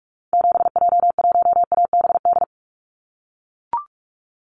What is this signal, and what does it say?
Morse code, heard as a keyed tone, about 44 wpm, 700 Hz. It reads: 7J1ABD
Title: Call Signs - 45wpm